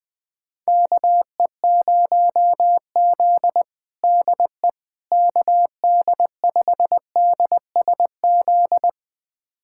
Morse code, heard as a keyed tone, about 21 wpm, 700 Hz. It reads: KE0Z DE KD5DSZ
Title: USA Callsign DE Callsign 20wpm